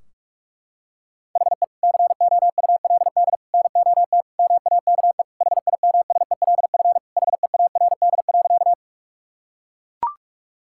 Morse code, heard as a keyed tone, about 41 wpm, 700 Hz. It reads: HE COULD NOT MAKE HIMSELF HEARD.
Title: Sentences from Top 1600 Words 45wpm